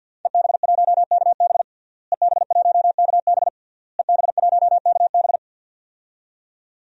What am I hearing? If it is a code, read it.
Morse code, heard as a keyed tone, about 47 wpm, 700 Hz. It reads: EB1KB EB1KB EB1KB
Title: DXCC-100 Call Signs - 5 Characters - Repeated 3x - 50wpm